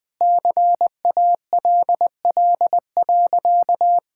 Morse code, heard as a keyed tone, about 21 wpm, 700 Hz. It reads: CALL.
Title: Sentences with 4-Word Phrases - 20wpm